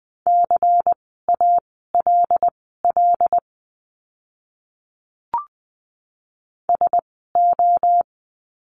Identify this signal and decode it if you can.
Morse code, heard as a keyed tone, about 20 wpm, spaced as for 14 wpm, 700 Hz. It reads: CALL SO
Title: 108 Lesson 18f - Letter m - Words 20wpm